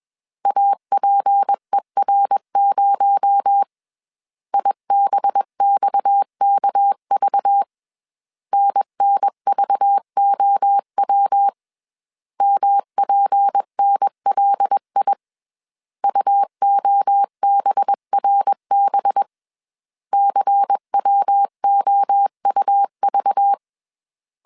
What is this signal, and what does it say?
Morse code, heard as a keyed tone, about 21 wpm, 800 Hz. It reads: APER0 IBXKV NNVOW MPNLI UOBRB CWOUV
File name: CW / DET / 1 6 1500C S A 20wpm 2048